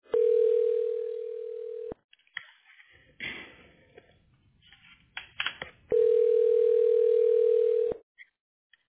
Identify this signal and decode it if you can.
Ringback tone (North America), 440 + 480 Hz.